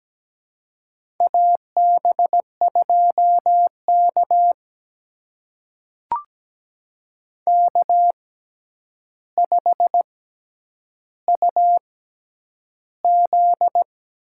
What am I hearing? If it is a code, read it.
Morse code, heard as a keyed tone, about 17 wpm, spaced as for 9 wpm, 700 Hz. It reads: AB2K K5UZ
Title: USA Call Signs - 4 Characters - 6x Character Spacing - 17wpm